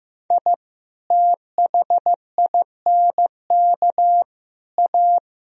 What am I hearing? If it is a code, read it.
Morse code, heard as a keyed tone, about 15 wpm, 700 Hz. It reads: I THINK A
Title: Speed-Racing - Sets of 5 Words 15wpm